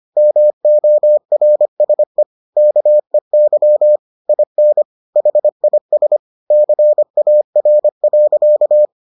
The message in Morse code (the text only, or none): MORSE KEY IN HIS CAR.